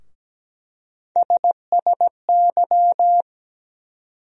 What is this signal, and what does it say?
Morse code, heard as a keyed tone, about 17 wpm, 700 Hz. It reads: SSY
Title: USA Call Sign Postfixes - 3 Letters 17wpm